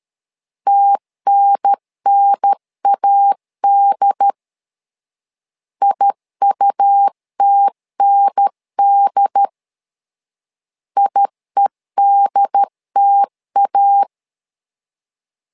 Morse code, heard as a keyed tone, about 13 wpm, 800 Hz. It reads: TNNAD IUTND IEDTA
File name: LEZIONI / 2 UHDN / 2 UHDN 12wpm 1456